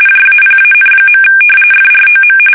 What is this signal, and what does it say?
RTTY, about 45.45 baud. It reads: JAPAN. FROM
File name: fsk850 / battleshipnj